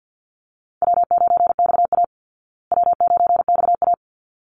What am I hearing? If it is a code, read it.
Morse code, heard as a keyed tone, about 50 wpm, 700 Hz. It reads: W9XA W9XA
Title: Easy Call Signs Repeated 2x - 50wpm